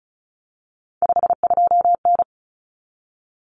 Morse code, heard as a keyed tone, about 34 wpm, 700 Hz. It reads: 52N
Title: Common Cut Numbers - 3-Digit 35wpm